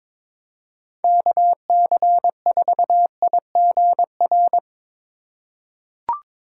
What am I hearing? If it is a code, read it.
Morse code, heard as a keyed tone, about 22 wpm, 700 Hz. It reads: KC4IGR
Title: USA Call Signs - 4x Character Spacing - 22wpm